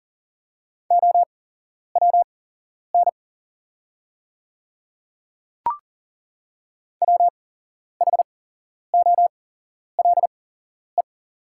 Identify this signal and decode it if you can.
Morse code, heard as a keyed tone, about 40 wpm, spaced as for 11 wpm, 700 Hz. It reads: OWN WHOLE
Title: 8x Character Spacing - Top 500 Words - 40wpm